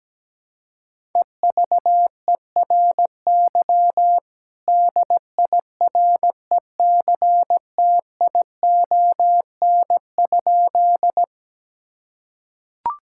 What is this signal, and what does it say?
Morse code, heard as a keyed tone, about 17 wpm, 700 Hz. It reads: EVERY DIRECTION?
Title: Sentences from Top 1000 Words - Review Part 5 - 17wpm